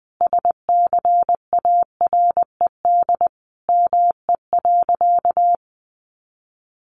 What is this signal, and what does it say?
Morse code, heard as a keyed tone, about 20 wpm, 700 Hz. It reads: SCARED ME.
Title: Sentences from Top 700 Words 20wpm